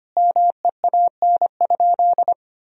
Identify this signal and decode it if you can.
Morse code, heard as a keyed tone, about 25 wpm, 700 Hz. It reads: MEAN?
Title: Sentences from Top 200 Words and Abbreviations 25wpm